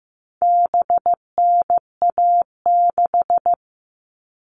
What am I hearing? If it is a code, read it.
Morse code, heard as a keyed tone, about 15 wpm, 700 Hz. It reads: BNA6
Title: All Cut Numbers - 4-Digit 15wpm